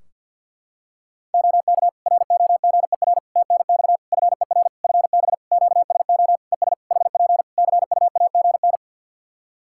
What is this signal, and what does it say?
Morse code, heard as a keyed tone, about 50 wpm, 700 Hz. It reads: OK ROGER TNX FER FB QSO ES HP CUAGN
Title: Practice QSO - Exchange 4 - Ending - 50wpm